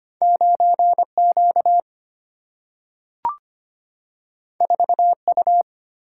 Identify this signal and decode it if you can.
Morse code, heard as a keyed tone, about 25 wpm, 700 Hz. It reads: 9Q 4U
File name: Two Letter-Number Combinations - 25wpm